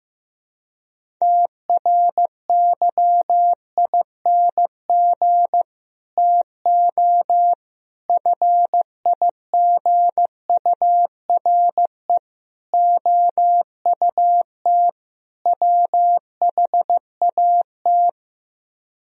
Morse code, heard as a keyed tone, about 15 wpm, 690 Hz. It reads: TRYING TO FIGURE OUT WHAT
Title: Speed-Racing - Sets of 5 Words 15wpm